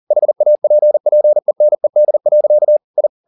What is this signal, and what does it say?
Morse code, heard as a keyed tone, about 40 wpm, 600 Hz. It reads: HAPPENED. I
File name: the old man and the sea 2 40wpm